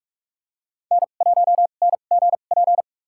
Morse code, heard as a keyed tone, about 46 wpm, spaced as for 32 wpm, 700 Hz. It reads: N1NGP
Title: USA Call Signs - 2x Character Spacing - 45wpm